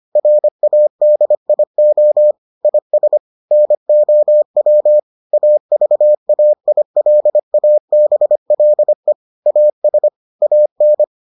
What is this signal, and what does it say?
Morse code, heard as a keyed tone, about 25 wpm, 600 Hz. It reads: RADIO IS NOW AVAILABLE AS AN